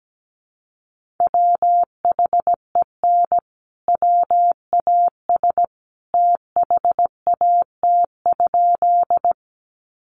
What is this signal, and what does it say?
Morse code, heard as a keyed tone, about 17 wpm, 700 Hz. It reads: WHEN WAS THAT?
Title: Sentences from Top 100 Words 17wpm